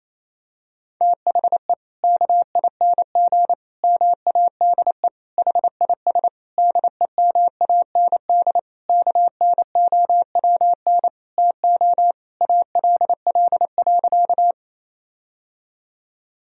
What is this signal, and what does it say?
Morse code, heard as a keyed tone, about 28 wpm, 700 Hz. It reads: THE KING MADE HIS DEMAND KNOWN TO ALL.
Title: Sentences from Top 1000 Words - Review Part 4 - 28wpm